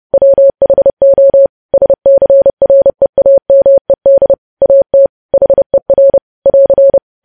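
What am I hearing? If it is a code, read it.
Morse code, heard as a keyed tone, about 30 wpm, 550 Hz. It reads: WHO SCREAMED AT HER <AR>
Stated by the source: News headlines sent in Morse at 30 wpm, preceded by VVV.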